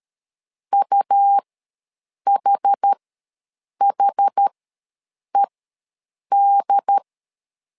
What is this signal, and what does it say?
Morse code, heard as a keyed tone, about 13 wpm, spaced as for 7 wpm, 800 Hz. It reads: UHHED
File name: LEZIONI / 2 UHDN / 2 UHD ws 12wpm 1722